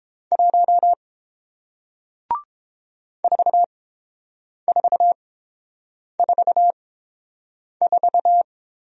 Morse code, heard as a keyed tone, about 30 wpm, spaced as for 7 wpm, 700 Hz. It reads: 1 4444